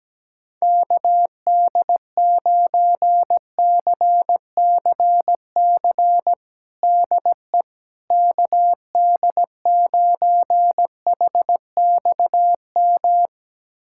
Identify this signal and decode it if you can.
Morse code, heard as a keyed tone, about 17 wpm, 700 Hz. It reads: KD9CCC DE KD9HXM
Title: USA Callsign DE Callsign - 6 Characters - 17wpm